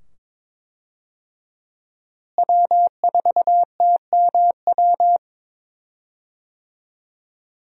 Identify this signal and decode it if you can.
Morse code, heard as a keyed tone, about 21 wpm, 700 Hz. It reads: W4TMW
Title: USA Call Signs - 5 Characters - Repeated 2x - 22wpm